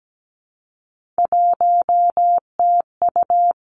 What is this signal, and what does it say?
Morse code, heard as a keyed tone, about 17 wpm, 700 Hz. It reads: 1TU
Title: Three Challenging Letter-Number Combinations 17wpm